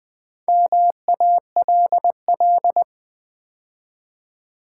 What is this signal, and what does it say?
Morse code, heard as a keyed tone, about 20 wpm, 700 Hz. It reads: MALL
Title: Build A Word - 2 Letters At A Time - 4-Letter Words - 20wpm